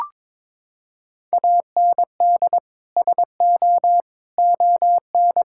Morse code, heard as a keyed tone, about 22 wpm, 700 Hz. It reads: AND SO ON
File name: practice / 22wpm / Sets of 6 Words - 22wpm